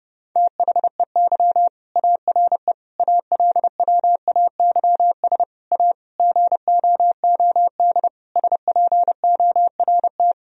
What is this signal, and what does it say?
Morse code, heard as a keyed tone, about 32 wpm, 700 Hz. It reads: THEY ARE ALWAYS A GOOD SPORT
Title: Sentences from Top 1000 Words - Review Part 4 - 30wpm